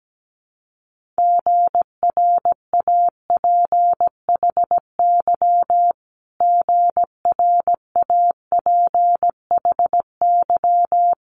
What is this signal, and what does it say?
Morse code, heard as a keyed tone, about 17 wpm, 700 Hz. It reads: GRAPHY GRAPHY
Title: Multi-letter Phonemes Repeated 3x 17wpm